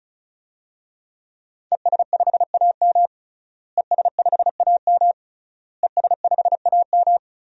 Morse code, heard as a keyed tone, about 34 wpm, 700 Hz. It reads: ES5AM ES5AM ES5AM
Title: DXCC-100 Call Signs - 5 Characters - Repeated 3x - 35wpm